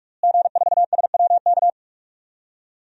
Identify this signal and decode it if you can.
Morse code, heard as a keyed tone, about 46 wpm, 700 Hz. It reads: G4SWK
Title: DXCC-100 Call Signs - 5 Characters - Repeated 3x - 45wpm